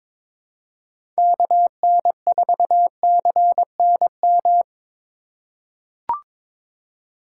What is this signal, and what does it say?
Morse code, heard as a keyed tone, about 22 wpm, 700 Hz. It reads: KN4CNM
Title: USA Call Signs - 6 Characters - 8x Character Spacing - 22wpm